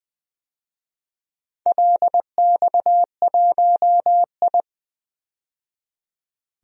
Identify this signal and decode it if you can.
Morse code, heard as a keyed tone, about 20 wpm, 700 Hz. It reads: LX1I
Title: DXCC-100 Call Signs - 4 Characters - 20wpm